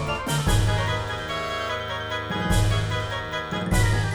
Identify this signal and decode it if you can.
There is no code.